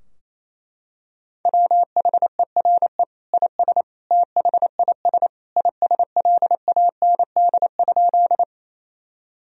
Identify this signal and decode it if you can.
Morse code, heard as a keyed tone, about 28 wpm, 700 Hz. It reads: WHERE IS THIS ISLAND?
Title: Sentences from Top 1000 Words - Review Part 3 - 28wpm